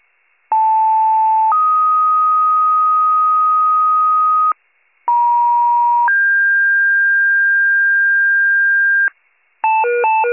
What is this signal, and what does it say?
Two-tone page, heard as a one-second tone, then a three-second tone, about 880 then 1250 Hz, about 950 then 1600 Hz, with noise. It is heard from 0.5 to 9.1 s.